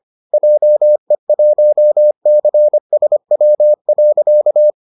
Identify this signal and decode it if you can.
Morse code, heard as a keyed tone, about 25 wpm, 600 Hz. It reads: JE1CSW.